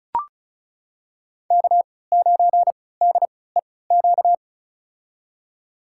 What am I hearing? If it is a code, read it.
Morse code, heard as a keyed tone, about 34 wpm, spaced as for 20 wpm, 700 Hz. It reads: K9DEQ
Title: USA Call Signs - 5 Characters - 3x Character Spacing - 35wpm